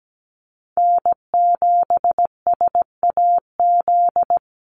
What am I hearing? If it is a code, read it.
Morse code, heard as a keyed tone, about 17 wpm, 700 Hz. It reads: N7SAZ
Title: USA Call Signs 17wpm